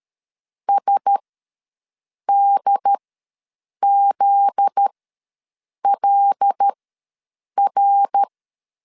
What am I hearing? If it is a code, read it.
Morse code, heard as a keyed tone, about 13 wpm, spaced as for 7 wpm, 800 Hz. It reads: SDZLR
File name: LEZIONI / 8 YJ45 / 8 Y ws 12wpm 3409